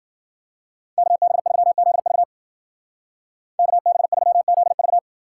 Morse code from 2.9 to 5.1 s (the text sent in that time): KB3ZV